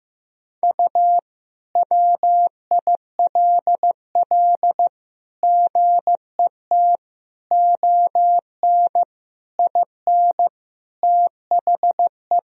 Morse code, heard as a keyed tone, about 15 wpm, 700 Hz. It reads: U WILL GET ON IN THE
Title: Sentences from Top 200 Words and Abbreviations 15wpm